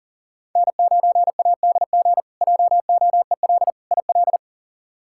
Morse code, heard as a keyed tone, about 40 wpm, 700 Hz. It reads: N9ADG JOEL IL
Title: K1USN SST QSO Practice - 40wpm